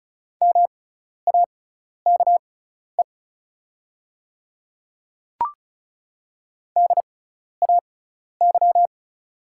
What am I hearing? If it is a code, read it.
Morse code, heard as a keyed tone, about 36 wpm, spaced as for 12 wpm, 690 Hz. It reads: MAKE DAY
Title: 6x Character Spacing - Top 200 Words - 35wpm